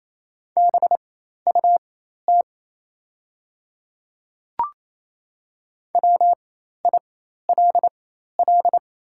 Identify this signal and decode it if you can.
Morse code, heard as a keyed tone, about 28 wpm, spaced as for 13 wpm, 700 Hz. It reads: BUT WILL
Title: 4x Character Spacing - Top 100 Words - 28wpm